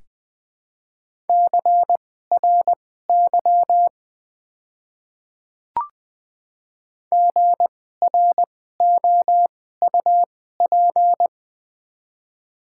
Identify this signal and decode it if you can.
Morse code, heard as a keyed tone, about 21 wpm, spaced as for 15 wpm, 690 Hz. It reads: CRY GROUP
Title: 2x Character Spacing - Top 500 Words - 20wpm